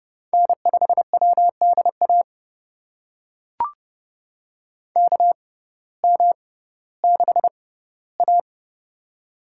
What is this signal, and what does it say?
Morse code, heard as a keyed tone, about 31 wpm, spaced as for 17 wpm, 700 Hz. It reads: N5WDA KM6A
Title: USA Call Signs - 6x Character Spacing - 30wpm